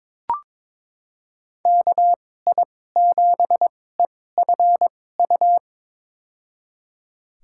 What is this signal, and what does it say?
Morse code, heard as a keyed tone, about 22 wpm, spaced as for 16 wpm, 700 Hz. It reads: KI7EFU
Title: USA Call Signs - 6 Characters - 2x Character Spacing - 22wpm